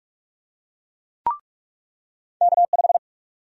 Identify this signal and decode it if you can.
Morse code, heard as a keyed tone, about 43 wpm, 700 Hz. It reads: K5
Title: USA Call Sign Prefixes - 45wpm